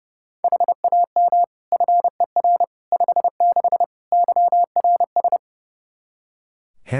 Morse code, heard as a keyed tone, about 31 wpm, 700 Hz. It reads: HAM FER 56 YRS
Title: Practice QSO - Exchange 3 - More Info - 30wpm